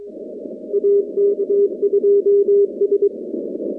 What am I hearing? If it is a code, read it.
Morse code, heard as a keyed tone, about 23 wpm, 400 Hz. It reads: AK2S
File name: contests / 2019-cw-wpx-kt5j-k5pi- -2330 0-45